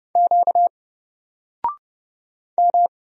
Morse code, heard as a keyed tone, about 30 wpm, 700 Hz. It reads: Q M
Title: Single Letter-Number Alt 30wpm